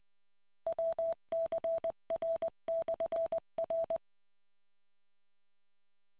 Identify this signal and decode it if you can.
Morse code, heard as a keyed tone, about 20 wpm, 660 Hz. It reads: WCR/R